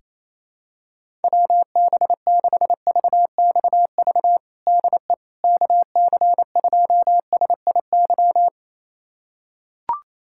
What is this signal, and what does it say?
Morse code, heard as a keyed tone, about 28 wpm, 700 Hz. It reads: WB6VXV DE KC2SIY
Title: USA Callsign DE Callsign 28wpm